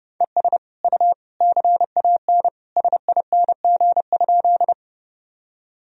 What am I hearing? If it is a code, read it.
Morse code, heard as a keyed tone, about 30 wpm, 700 Hz. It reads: ES U CAN SING?
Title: Sentences from Top 400 Words and Abbreviations 30wpm